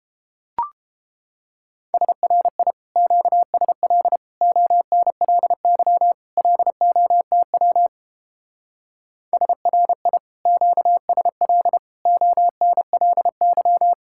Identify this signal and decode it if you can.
Morse code, heard as a keyed tone, about 29 wpm, 700 Hz. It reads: SRI QSL ONLY LOTW SRI QSL ONLY